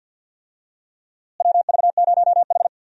Morse code, heard as a keyed tone, about 50 wpm, 700 Hz. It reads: WV0H
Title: USA Call Signs - 4 Characters - Repeated 3x - 50wpm